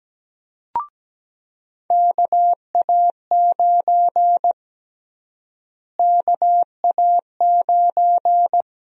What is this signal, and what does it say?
Morse code, heard as a keyed tone, about 17 wpm, 700 Hz. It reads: KA9 KA9
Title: USA Call Sign Prefixes - Repeated 2x - 17wpm